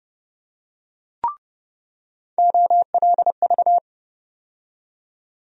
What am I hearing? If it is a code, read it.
Morse code, heard as a keyed tone, about 30 wpm, 700 Hz. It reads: OLV
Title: USA Call Sign Postfixes - 3 Letters 30wpm